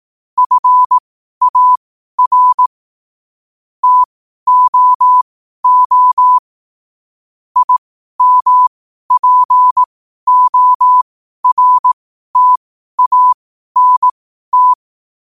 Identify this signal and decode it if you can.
Morse code, heard as a keyed tone, about 18 wpm, spaced as for 12 wpm, 1000 Hz. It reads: FAR TOO IMPORTANT
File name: qotdfiles / 12wpm